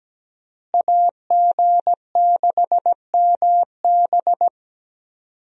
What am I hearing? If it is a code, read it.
Morse code, heard as a keyed tone, about 17 wpm, 700 Hz. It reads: AG6MB
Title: USA Call Signs - 5 Characters - Repeated 2x - 17wpm